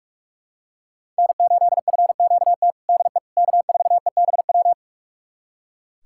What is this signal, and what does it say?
Morse code, heard as a keyed tone, about 46 wpm, 700 Hz. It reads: N8FQT DE K4EBW
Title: USA Callsign DE Callsign - 5 Characters - 45wpm